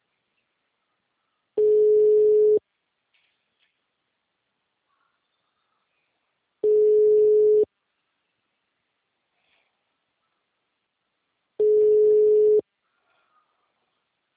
Ringback tone (Europe, 425 Hz).
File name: out- -2015